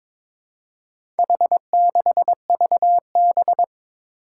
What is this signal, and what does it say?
Morse code, heard as a keyed tone, about 22 wpm, 700 Hz. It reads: H6VB
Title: Four Challenging Letter-Number Combinations 22wpm